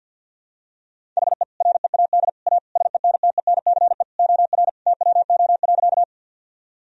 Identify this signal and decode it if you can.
Morse code, heard as a keyed tone, about 52 wpm, 700 Hz. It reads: HE READ A SENTENCE OR TWO.
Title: Sentences from Top 1000 Words - Review Part 1 - 50wpm